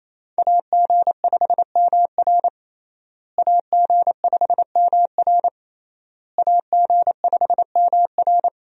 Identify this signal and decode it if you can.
Morse code, heard as a keyed tone, about 27 wpm, 700 Hz. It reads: AG5MR AG5MR AG5MR
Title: USA Call Signs - 5 Characters - Repeated 3x - 28wpm